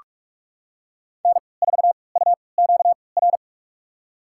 Morse code, heard as a keyed tone, about 42 wpm, spaced as for 25 wpm, 700 Hz. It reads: N4UQR
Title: USA Call Signs - 5 Characters - 3x Character Spacing - 45wpm